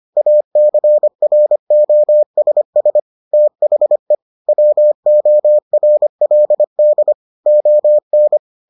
Morse code, heard as a keyed tone, about 25 wpm, 600 Hz. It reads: ACROSS THE WORLD ON